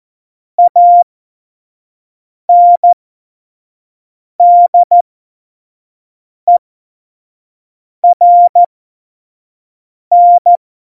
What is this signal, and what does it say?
Morse code, heard as a keyed tone, about 14 wpm, spaced as for 5 wpm, 700 Hz. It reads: ANDERN
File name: Morsekurs / Witze / Witz 17. Oktober 17, Tempo 25